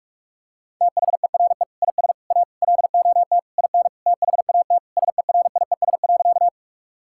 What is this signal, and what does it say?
Morse code, heard as a keyed tone, about 46 wpm, 700 Hz. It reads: THERE IS A LOT IN THAT SERIES.
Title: Sentences from Top 1000 Words - Review Part 5 - 45wpm